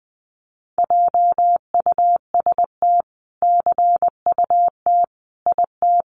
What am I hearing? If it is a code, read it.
Morse code, heard as a keyed tone, about 20 wpm, 700 Hz. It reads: JUST CUT IT
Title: Sentences from Top 300 Words 20wpm